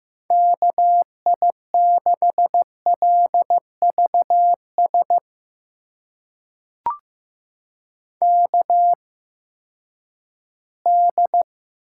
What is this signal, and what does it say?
Morse code, heard as a keyed tone, about 15 wpm, spaced as for 10 wpm, 700 Hz. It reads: KI6LVS KD
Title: USA Call Signs - 6 Characters - 8x Character Spacing - 15wpm